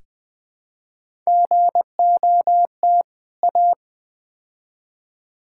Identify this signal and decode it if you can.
Morse code, heard as a keyed tone, about 20 wpm, 700 Hz. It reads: GOT A
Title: Build A Phrase - 4-Word Phrases - 20wpm